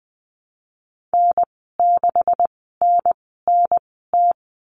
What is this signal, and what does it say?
Morse code, heard as a keyed tone, about 20 wpm, spaced as for 14 wpm, 700 Hz. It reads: N6NNT
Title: 154 Lesson 25g - Letter p - Call Signs 20wpm